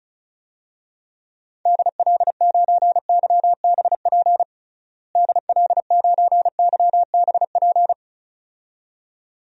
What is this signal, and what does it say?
Morse code, heard as a keyed tone, about 34 wpm, 700 Hz. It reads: DL9YBP DL9YBP
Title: DXCC-100 Call Signs - 6 Characters - Repeated 2x - 35wpm